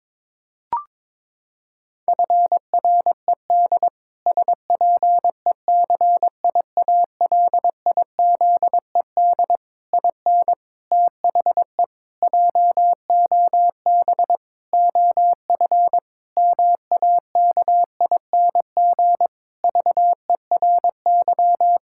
Morse code, heard as a keyed tone, about 22 wpm, 700 Hz. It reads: FRED SPECIALIZED IN THE JOB OF MAKING VERY
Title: Pangram Challenge v2 - 22wpm